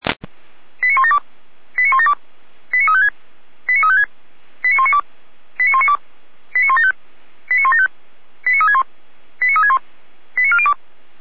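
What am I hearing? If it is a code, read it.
Five-tone selective call (ZVEI), heard as quick runs of five tones, tones of about 70 ms. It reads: 89172 89172 89356 89356 89192 89192 89175 89175 89361 89361 89402
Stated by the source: ZVEI five-tone call 89370 received by SDR.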